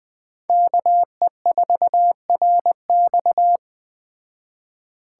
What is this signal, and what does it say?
Morse code, heard as a keyed tone, about 20 wpm, 700 Hz. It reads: KE4RX
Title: Speed Racing - USA 2x2 Call Signs 17wpm